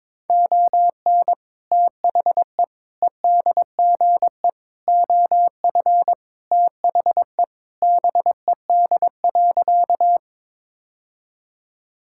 Morse code, heard as a keyed tone, about 21 wpm, 700 Hz. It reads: ON THE EDGE OF THE BED.
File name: practice / 22wpm / Sentences with 3-Word Phrases - 22wpm